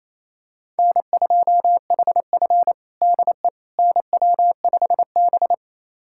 Morse code, heard as a keyed tone, about 28 wpm, 700 Hz. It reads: N2HF DE NW5B
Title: USA Callsign DE Callsign - 4 Characters - 28wpm